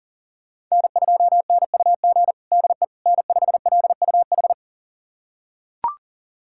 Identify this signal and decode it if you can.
Morse code, heard as a keyed tone, about 40 wpm, 700 Hz. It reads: N2NUG DE N5LUH
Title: USA Callsign DE Callsign - 5 Characters - 40wpm